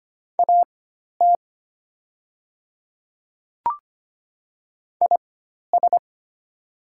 Morse code, heard as a keyed tone, about 25 wpm, spaced as for 12 wpm, 700 Hz. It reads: AT IS
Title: 4x Character Spacing - Top 100 Words - 25wpm